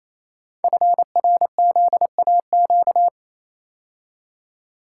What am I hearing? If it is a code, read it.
Morse code, heard as a keyed tone, about 29 wpm, 700 Hz. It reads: FRZAQ
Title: Code Groups - 5 Letters - 28wpm